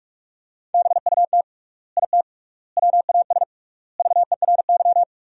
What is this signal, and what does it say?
Morse code, heard as a keyed tone, about 41 wpm, 700 Hz. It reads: BUT IT WAS VERY
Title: 3x Word-Spacing - Sentences from Top 300 Words - 45wpm